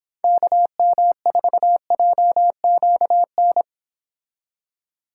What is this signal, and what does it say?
Morse code, heard as a keyed tone, about 26 wpm, 700 Hz. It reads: KM4JQN